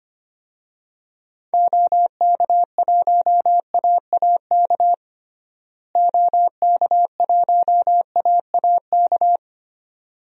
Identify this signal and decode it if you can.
Morse code, heard as a keyed tone, about 25 wpm, 700 Hz. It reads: OK1AAK OK1AAK
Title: DXCC-100 Call Signs - 6 Characters - Repeated 2x - 25wpm